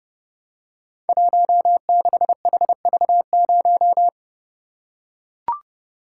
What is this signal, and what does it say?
Morse code, heard as a keyed tone, about 29 wpm, 700 Hz. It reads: 16HV0
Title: Code Groups - 5 Letter:Numbers - 30wpm